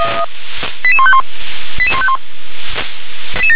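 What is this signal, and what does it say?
Five-tone selective call (ZVEI), heard as quick runs of five tones, tones of about 70 ms. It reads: 88151 88151
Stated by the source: Received by SDR as a ZVEI five-tone selective call.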